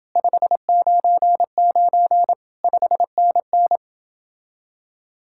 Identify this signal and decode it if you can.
Morse code, heard as a keyed tone, about 28 wpm, 700 Hz. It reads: 599 5NN